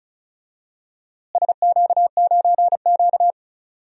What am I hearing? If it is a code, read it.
Morse code, heard as a keyed tone, about 34 wpm, 700 Hz. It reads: SQ9Q
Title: DXCC-100 Call Signs - 4 Characters - Repeated 2x - 35wpm